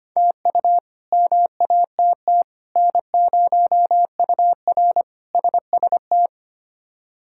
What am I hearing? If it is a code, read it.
Morse code, heard as a keyed tone, about 25 wpm, 700 Hz. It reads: TU MATT N0UR SST
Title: K1USN SST QSO Practice - 25wpm